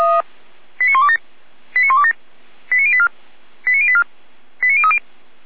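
Five-tone selective call (ZVEI), heard as quick runs of five tones, tones of about 70 ms. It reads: 89127 89127 89084 89084 89030
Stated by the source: Five-tone ZVEI selective call received on an SDR.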